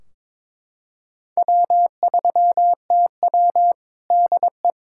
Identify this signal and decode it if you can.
Morse code, heard as a keyed tone, about 22 wpm, 700 Hz. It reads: W3TW DE
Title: K1USN SST QSO Practice 22wpm